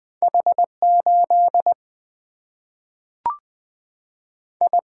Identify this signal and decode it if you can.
Morse code, heard as a keyed tone, about 20 wpm, 700 Hz. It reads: H8 I
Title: Single-Letter Single-Number 20wpm